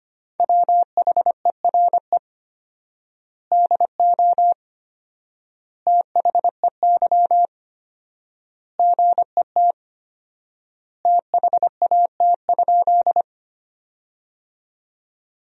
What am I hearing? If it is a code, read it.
Morse code, heard as a keyed tone, about 25 wpm, 700 Hz. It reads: WHERE DO THEY GET THAT?
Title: 4x Word-Spacing - Sentences from Top 200 Words - 25wpm